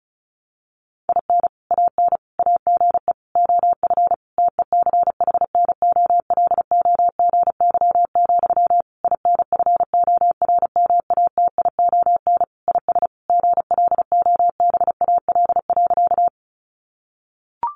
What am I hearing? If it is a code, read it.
Morse code, heard as a keyed tone, about 34 wpm, 700 Hz. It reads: IN AN AGE OF TECHNOLOGY, INFORMATION IS GLOBAL.
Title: Sentences from Top 1200 Words 35wpm